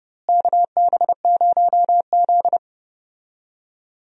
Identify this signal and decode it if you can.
Morse code, heard as a keyed tone, about 30 wpm, 700 Hz. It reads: KB0Z
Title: USA Call Signs - 4 Characters - Repeated 2x - 30wpm